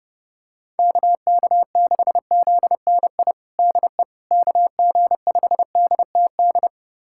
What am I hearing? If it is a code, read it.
Morse code, heard as a keyed tone, about 31 wpm, 700 Hz. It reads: KK6ZNI DE KG5DTD
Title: USA Callsign DE Callsign - 6 Characters - 30wpm